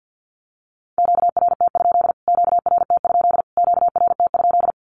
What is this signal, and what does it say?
Morse code, heard as a keyed tone, about 50 wpm, 700 Hz. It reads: QRT? QRT? QRT?
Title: Q-Codes Repeated 3x 50wpm